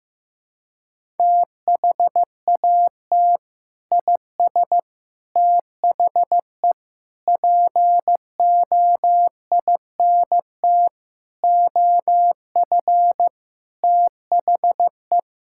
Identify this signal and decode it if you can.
Morse code, heard as a keyed tone, about 16 wpm, 700 Hz. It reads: THAT IS THE POINT OF THE
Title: Sentences from Top 1000 Words - Review Part 5 - 15wpm